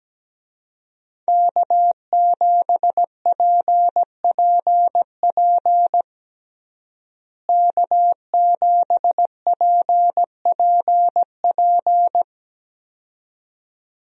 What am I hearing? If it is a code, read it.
Morse code, heard as a keyed tone, about 17 wpm, 700 Hz. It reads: K7PPP K7PPP
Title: USA Call Signs - 5 Characters - Repeated 2x - 17wpm